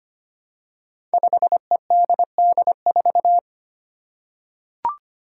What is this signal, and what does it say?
Morse code, heard as a keyed tone, about 25 wpm, 700 Hz. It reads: 5EDD4
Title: Five Challenging Letter-Number Combinations 25wpm